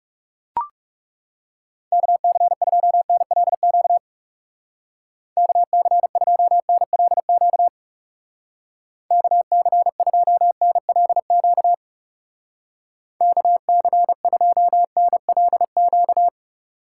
Morse code, heard as a keyed tone, about 37 wpm, 700 Hz. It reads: KC2NLQ KC2NLQ KC2NLQ KC2NLQ
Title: Speed Racing - USA 2x3 Call Signs 30wpm